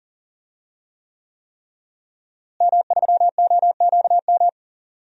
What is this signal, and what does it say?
Morse code, heard as a keyed tone, about 38 wpm, 700 Hz. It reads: M3OQM
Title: DXCC-100 Call Signs - 5 Characters - Repeated 3x - 40wpm